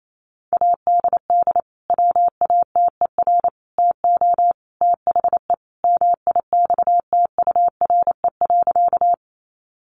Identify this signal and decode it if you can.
Morse code, heard as a keyed tone, about 28 wpm, 700 Hz. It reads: ADD WATER TO THE MIXTURE.
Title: Sentences from Top 1500 Words 28wpm